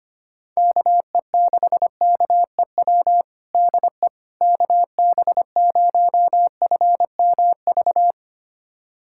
Morse code, heard as a keyed tone, about 25 wpm, 700 Hz. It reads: KE6KEW DE KB0FMV
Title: USA Callsign DE Callsign 25wpm